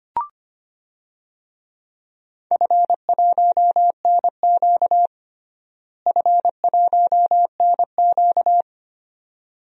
Morse code, heard as a keyed tone, about 25 wpm, 700 Hz. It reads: F1NQ F1NQ
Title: DXCC-100 Call Signs - 4 Characters - Repeated 2x - 25wpm